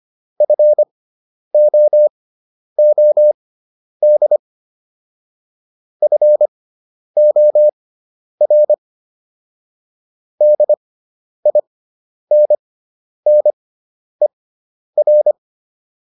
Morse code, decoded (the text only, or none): FOOD FOR DINNER